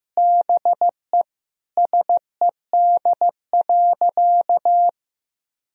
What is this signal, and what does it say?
Morse code, heard as a keyed tone, about 15 wpm, 700 Hz. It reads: BE SED.
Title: Sentences from Top 100 Words and Abbreviations 15wpm